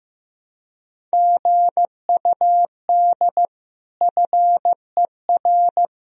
Morse code, heard as a keyed tone, about 15 wpm, 700 Hz. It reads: GUD FER
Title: Sentences from Top 300 Words and Abbreviations 15wpm